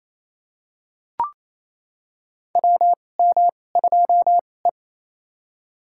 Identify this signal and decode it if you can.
Morse code, heard as a keyed tone, about 28 wpm, spaced as for 20 wpm, 700 Hz. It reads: WM2E
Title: USA Call Signs - 4 Characters - 2x Character Spacing - 28wpm